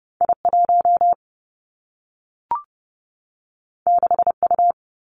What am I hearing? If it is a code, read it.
Morse code, heard as a keyed tone, about 30 wpm, 700 Hz. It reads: I 1 6 U
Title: Two Challenging Letter-Number Combinations 30wpm